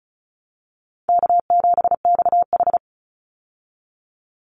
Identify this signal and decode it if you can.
Morse code, heard as a keyed tone, about 34 wpm, 700 Hz. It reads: K7XH
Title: Call Signs Repeated 2x - 35wpm